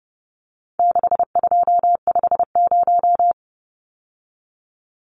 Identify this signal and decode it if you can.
Morse code, heard as a keyed tone, about 30 wpm, 700 Hz. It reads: 6250
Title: Numbers - 4-Digit 30wpm